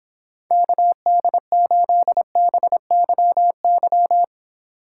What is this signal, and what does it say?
Morse code, heard as a keyed tone, about 26 wpm, 700 Hz. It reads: KD8BYY